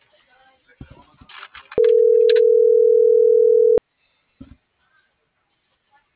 Ringback tone (North America), 440 + 480 Hz.